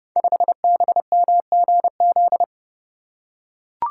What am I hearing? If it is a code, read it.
Morse code, heard as a keyed tone, about 29 wpm, 700 Hz. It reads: 5BMGZ
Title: Code Groups - 5 Letter:Numbers - 30wpm